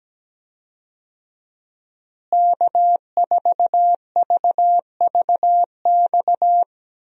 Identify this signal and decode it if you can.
Morse code, heard as a keyed tone, about 17 wpm, 700 Hz. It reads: K4VVX
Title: USA Call Signs - 5 Characters - Repeated 3x - 17wpm